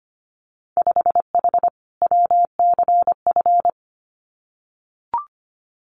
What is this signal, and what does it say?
Morse code, heard as a keyed tone, about 25 wpm, 700 Hz. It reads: 5H WCF
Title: Winter Field Day Exchanges Encore 25wpm